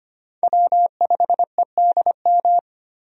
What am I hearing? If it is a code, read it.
Morse code, heard as a keyed tone, about 25 wpm, 700 Hz. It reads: W5EDM
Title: USA Call Signs - 5 Characters - 25wpm